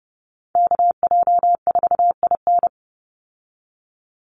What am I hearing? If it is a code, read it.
Morse code, heard as a keyed tone, about 29 wpm, 700 Hz. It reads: KJ4IN
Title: USA Call Signs 30wpm